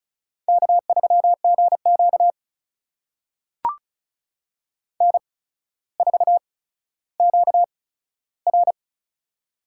Morse code, heard as a keyed tone, about 36 wpm, spaced as for 15 wpm, 700 Hz. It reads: K3GQ N4QR
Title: USA Call Signs - 4 Characters - 8x Character Spacing - 35wpm